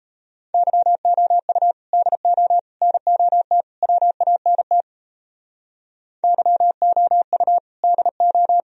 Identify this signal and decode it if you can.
Morse code, heard as a keyed tone, about 36 wpm, 700 Hz. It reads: YOU DO NOT WANT YOU DO